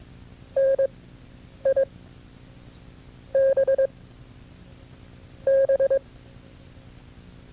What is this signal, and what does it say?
Morse code, heard as a keyed tone, about 22 wpm, spaced as for 6 wpm, 540 Hz. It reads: NIBB
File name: FreeFiles / CW-practice / KB6MT-tape2-side1